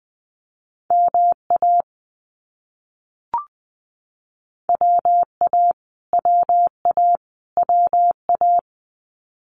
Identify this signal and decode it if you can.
Morse code, heard as a keyed tone, about 20 wpm, 700 Hz. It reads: MA WA WA WA
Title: US States Repeated 3x 20wpm